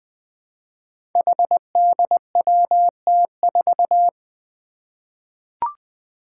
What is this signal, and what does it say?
Morse code, heard as a keyed tone, about 20 wpm, 700 Hz. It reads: HDWT4
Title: Five Challenging Letter-Number Combinations 20wpm